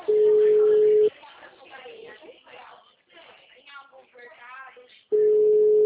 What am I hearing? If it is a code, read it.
Ringback tone (Europe, 425 Hz), with speech.